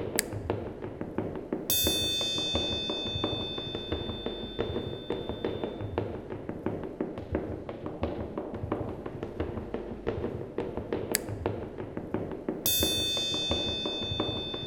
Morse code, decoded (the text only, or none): none